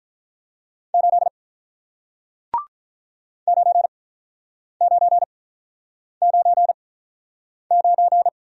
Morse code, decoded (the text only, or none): 8 9999